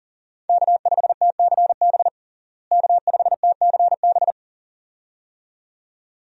Morse code, heard as a keyed tone, about 40 wpm, 700 Hz. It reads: K5TCB K5TCB
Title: USA Call Signs - 5 Characters - Repeated 2x - 40wpm